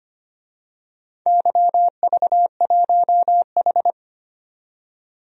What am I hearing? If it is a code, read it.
Morse code, heard as a keyed tone, about 25 wpm, 700 Hz. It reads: YV1H
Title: DXCC-100 Call Signs - 4 Characters - 25wpm